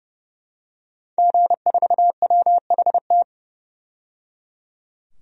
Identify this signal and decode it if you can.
Morse code, heard as a keyed tone, about 31 wpm, 700 Hz. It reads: G4WHT
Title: DXCC-100 Call Signs - 5 Characters - 30wpm